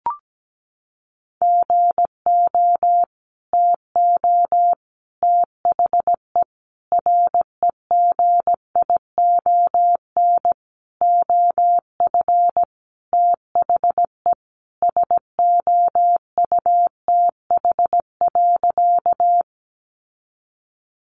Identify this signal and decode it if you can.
Morse code, heard as a keyed tone, about 17 wpm, 700 Hz. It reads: GO TO THE REGION OF THE SOUTH.
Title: Sentences from Top 1500 Words 17wpm